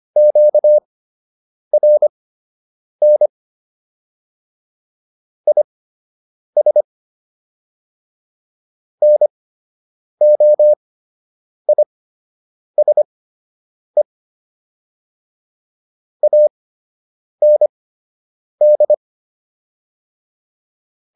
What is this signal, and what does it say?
Morse code, heard as a keyed tone, about 25 wpm, spaced as for 8 wpm, 600 Hz. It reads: QRN IS NOISE AND